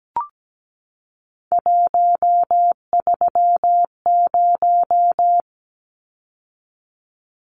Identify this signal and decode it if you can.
Morse code, heard as a keyed tone, about 17 wpm, 700 Hz. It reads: 130
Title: Numbers - 3-Digit 17wpm